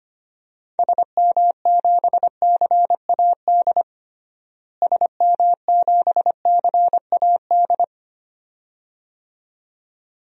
Morse code, heard as a keyed tone, about 25 wpm, 700 Hz. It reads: SM7CAD SM7CAD
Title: DXCC-100 Call Signs - 6 Characters - Repeated 2x - 25wpm